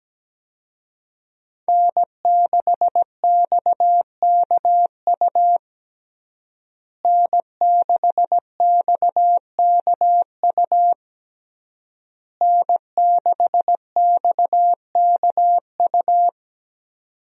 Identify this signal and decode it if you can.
Morse code, heard as a keyed tone, about 17 wpm, 700 Hz. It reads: N6XKU N6XKU N6XKU
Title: USA Call Signs - 5 Characters - Repeated 3x - 17wpm